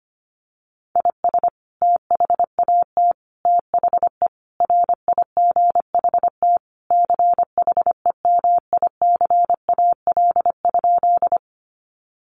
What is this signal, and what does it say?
Morse code, heard as a keyed tone, about 25 wpm, 700 Hz. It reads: IS THAT THE RIGHT CHEMICAL?
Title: Sentences from Top 1200 Words 25wpm